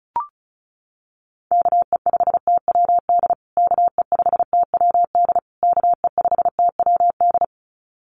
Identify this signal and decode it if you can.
Morse code, heard as a keyed tone, about 34 wpm, 700 Hz. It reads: KE5TWD KE5TWD KE5TWD
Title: USA Call Signs Repeated 3x 35wpm